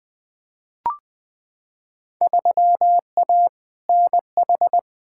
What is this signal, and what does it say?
Morse code, heard as a keyed tone, about 20 wpm, 700 Hz. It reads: 3A NH
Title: Field Day Exchanges - 20wpm